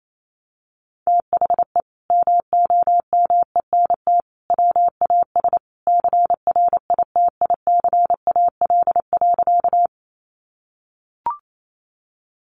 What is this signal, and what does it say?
Morse code, heard as a keyed tone, about 28 wpm, 700 Hz. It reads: THE MOMENT WAS CRITICAL.
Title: Sentences from Top 1100 Words 28wpm